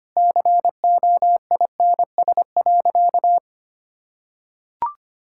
Morse code, heard as a keyed tone, about 25 wpm, 700 Hz. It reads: COINS.
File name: Easy Sentences with Hints - 25wpm